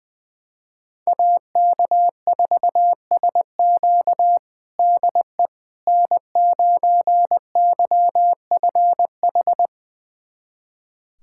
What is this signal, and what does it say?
Morse code, heard as a keyed tone, about 20 wpm, 700 Hz. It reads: AK4SQ DE N9YFH
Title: USA Callsign DE Callsign - 5 Characters - 20wpm